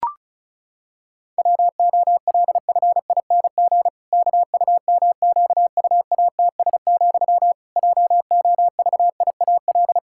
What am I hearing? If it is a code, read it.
Morse code, heard as a keyed tone, about 36 wpm, 700 Hz. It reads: WOLFING KUMQUATS, JOVIAL
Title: Pangram Challenge v2 - 35wpm